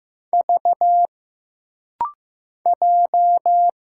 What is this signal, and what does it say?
Morse code, heard as a keyed tone, about 15 wpm, 700 Hz. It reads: V J
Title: Single Letter-Number Alt - Rapid-Fire 15wpm